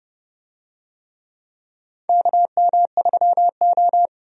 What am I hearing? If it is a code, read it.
Morse code, heard as a keyed tone, about 30 wpm, 700 Hz. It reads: KM3O
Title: USA Call Signs - 4 Characters - 30wpm